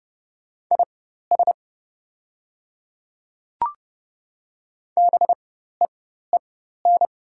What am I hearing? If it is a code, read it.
Morse code, heard as a keyed tone, about 30 wpm, spaced as for 14 wpm, 700 Hz. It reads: IS BEEN
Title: 4x Character Spacing - Top 200 Words - 30wpm